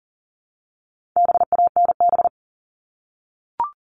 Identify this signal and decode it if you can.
Morse code, heard as a keyed tone, about 40 wpm, 700 Hz. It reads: BANB
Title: All Cut Numbers - 4-Digit 40wpm